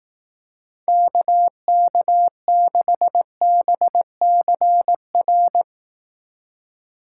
Morse code, heard as a keyed tone, about 18 wpm, 700 Hz. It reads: KK6BCR
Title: Speed Racing - USA 2x3 Call Signs 15wpm